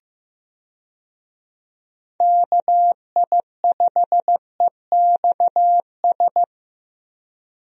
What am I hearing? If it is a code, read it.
Morse code, heard as a keyed tone, about 15 wpm, 700 Hz. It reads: KI5EXS
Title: USA Call Signs - 6 Characters - Repeated 3x - 15wpm